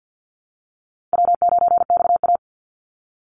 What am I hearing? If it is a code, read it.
Morse code, heard as a keyed tone, about 50 wpm, 700 Hz. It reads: W9XA
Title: Call Signs Repeated 3x - 50wpm